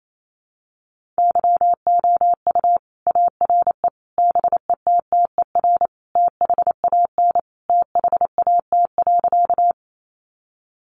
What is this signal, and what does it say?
Morse code, heard as a keyed tone, about 28 wpm, 700 Hz. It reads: YOU ARE BETTER THAN THAT.
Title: Sentences from Top 400 Words 28wpm